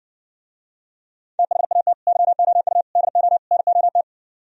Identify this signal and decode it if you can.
Morse code, heard as a keyed tone, about 69 wpm, 700 Hz. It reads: THAT YOU DO NOT